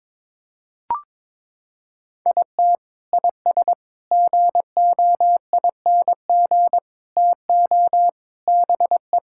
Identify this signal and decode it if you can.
Morse code, heard as a keyed tone, about 22 wpm, 700 Hz. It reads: IT IS GOING TO BE
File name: practice / 22wpm / Sentences with 4-Word Phrases - 22wpm